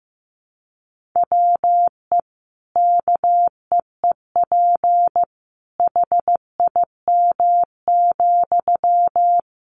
Morse code, heard as a keyed tone, about 15 wpm, 700 Hz. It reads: WE KEEP HIM,
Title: Sentences from Top 600 Words 15wpm